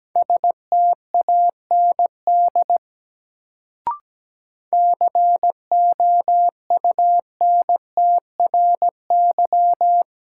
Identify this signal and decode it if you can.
Morse code, heard as a keyed tone, about 17 wpm, 700 Hz. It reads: STAND COUNTRY
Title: Sentences from Top 300 Words - New Word Drill - 17wpm